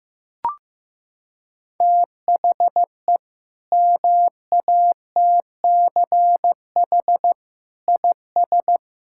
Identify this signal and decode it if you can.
Morse code, heard as a keyed tone, about 15 wpm, 700 Hz. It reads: THE MATCH IS
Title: Sentences from Top 1600 Words 15wpm